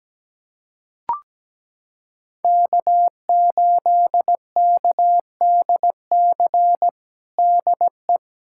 Morse code, heard as a keyed tone, about 17 wpm, 700 Hz. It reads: K8KDC DE
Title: USA Callsign DE Callsign - 5 Characters - 17wpm